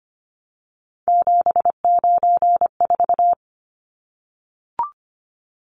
Morse code, heard as a keyed tone, about 25 wpm, 700 Hz. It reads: 794
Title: Numbers - 3-Digit 25wpm